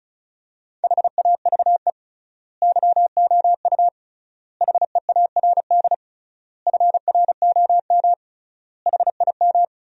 Morse code, heard as a keyed tone, about 34 wpm, 700 Hz. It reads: HAVE YOU HEARD FROM HIM
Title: 3x Word-Spacing - Sentences from Top 400 Words - 35wpm